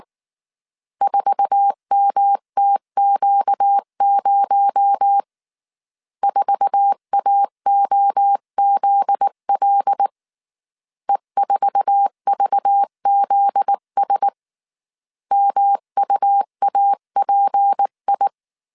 Morse code, heard as a keyed tone, about 19 wpm, 800 Hz. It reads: 4MTQ0 4AOZL E4VZS MUAPI